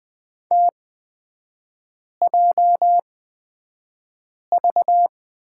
Morse code, heard as a keyed tone, about 20 wpm, spaced as for 5 wpm, 700 Hz. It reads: TJV
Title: 236 Lesson 38e - Letter q - Three Characters 20wpm